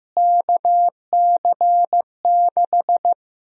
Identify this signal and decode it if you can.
Morse code, heard as a keyed tone, about 16 wpm, 700 Hz. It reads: KC6
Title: USA Call Sign Prefixes - Repeated 2x - 15wpm